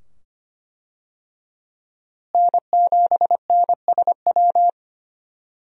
Morse code, heard as a keyed tone, about 25 wpm, 700 Hz. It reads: N7NSW
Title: USA Call Signs - 5 Characters - Repeated 2x - 25wpm